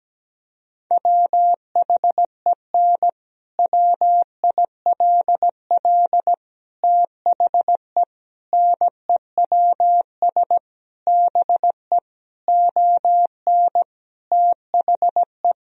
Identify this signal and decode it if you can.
Morse code, heard as a keyed tone, about 17 wpm, 700 Hz. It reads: WHEN WILL THE NEWS BE ON THE
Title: Sentences from Top 1000 Words - Review Part 3 - 17wpm